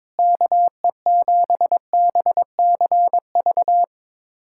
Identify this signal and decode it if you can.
Morse code, heard as a keyed tone, about 22 wpm, 700 Hz. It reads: KE7BCV
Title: USA Callsign DE Callsign - 6 Characters - 22wpm